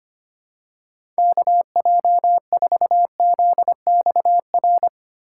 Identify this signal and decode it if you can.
Morse code, heard as a keyed tone, about 25 wpm, 700 Hz. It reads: KJ4ZXR
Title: USA Call Signs - 4x Character Spacing - 25wpm